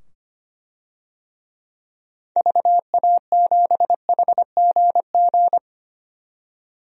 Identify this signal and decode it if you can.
Morse code, heard as a keyed tone, about 25 wpm, 700 Hz. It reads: VA7HGG
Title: DXCC-100 Call Signs - 6 Characters - 25wpm